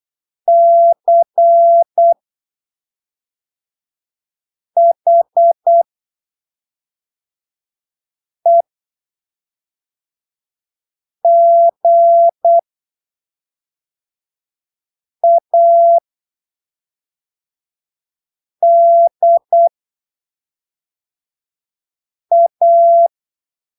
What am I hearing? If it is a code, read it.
Morse code, heard as a keyed tone, about 8 wpm, spaced as for 3 wpm, 680 Hz. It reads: CHEGADA
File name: arquivos / texto / texto 19 08ppm